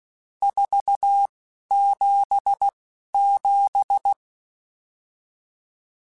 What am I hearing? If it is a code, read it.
Morse code, heard as a keyed tone, about 16 wpm, spaced as for 11 wpm, 800 Hz. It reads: 477